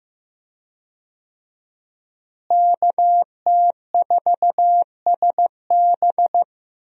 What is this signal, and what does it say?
Morse code, heard as a keyed tone, about 15 wpm, 700 Hz. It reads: KT4SB
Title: USA Call Signs - 5 Characters - Repeated 3x - 15wpm